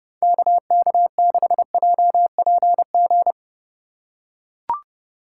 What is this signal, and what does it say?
Morse code, heard as a keyed tone, about 30 wpm, 700 Hz. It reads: KK6JPG